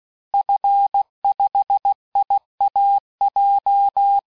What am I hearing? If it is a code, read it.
Morse code, heard as a keyed tone, about 16 wpm, 800 Hz. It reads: F5IAJ
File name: cours-du-lundi files / cours-1580-fin-QSO